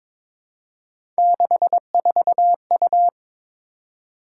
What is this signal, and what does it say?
Morse code, heard as a keyed tone, about 22 wpm, 700 Hz. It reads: 64U
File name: practice / 22wpm / Three Letter-Number Combinations - 22wpm